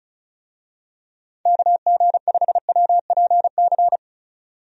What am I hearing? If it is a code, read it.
Morse code, heard as a keyed tone, about 34 wpm, 700 Hz. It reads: KG5WPC
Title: USA Call Signs - 6 Characters - 35wpm